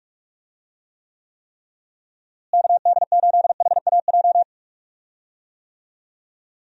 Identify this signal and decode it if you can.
Morse code, heard as a keyed tone, about 46 wpm, 700 Hz. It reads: KD8HAJ
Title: USA Call Signs - 6 Characters - Repeated 3x - 45wpm